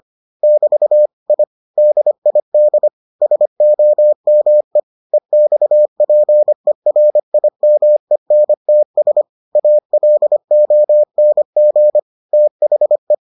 Morse code, heard as a keyed tone, about 25 wpm, 600 Hz. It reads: <BT> I DID SOME EXPERIMENTS ALONG THE